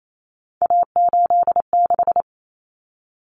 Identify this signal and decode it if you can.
Morse code, heard as a keyed tone, about 28 wpm, 700 Hz. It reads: A86
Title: Common Cut Numbers - 3-Digit 28wpm